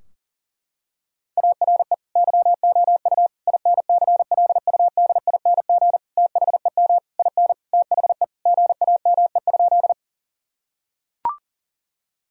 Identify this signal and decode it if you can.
Morse code, heard as a keyed tone, about 37 wpm, 700 Hz. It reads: ARE YOU INCLUDING THEM IN THE GAME?
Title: Sentences from Top 1000 Words - Review Part 4 - 40wpm